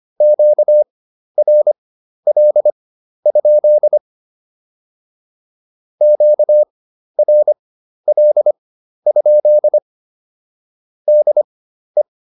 Morse code, heard as a keyed tone, about 25 wpm, spaced as for 12 wpm, 600 Hz. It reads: QRL? QRL? DE
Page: fundamental-practice-files